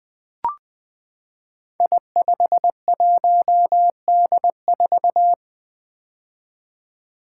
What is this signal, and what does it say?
Morse code, heard as a keyed tone, about 20 wpm, 700 Hz. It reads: I51D4
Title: Code Groups - 5 Letter:Numbers - 20wpm